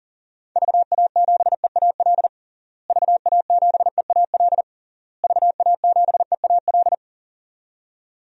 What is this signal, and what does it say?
Morse code, heard as a keyed tone, about 40 wpm, 700 Hz. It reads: VA7EAL VA7EAL VA7EAL
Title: DXCC-100 Call Signs - 6 Characters - Repeated 3x - 40wpm